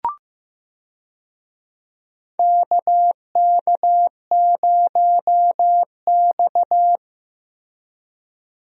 Morse code, heard as a keyed tone, about 16 wpm, 700 Hz. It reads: KK0X
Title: USA Call Signs - 4 Characters - Repeated 3x - 15wpm